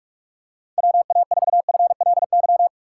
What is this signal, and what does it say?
Morse code, heard as a keyed tone, about 43 wpm, 700 Hz. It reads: WA4FLY
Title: USA Call Signs - 6 Characters - 2x Character Spacing - 45wpm